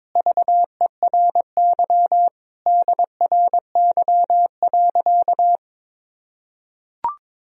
Morse code, heard as a keyed tone, about 22 wpm, 700 Hz. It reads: VERY DRY.
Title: Sentences from Top 1000 Words - Review Part 3 - 22wpm